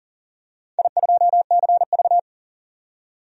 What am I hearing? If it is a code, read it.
Morse code, heard as a keyed tone, about 40 wpm, 700 Hz. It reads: I2CV
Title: DXCC-100 Call Signs - 4 Characters - 40wpm